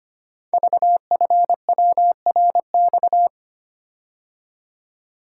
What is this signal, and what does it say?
Morse code, heard as a keyed tone, about 25 wpm, 700 Hz. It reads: VFWRX
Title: Code Groups - 5 Letters - 25wpm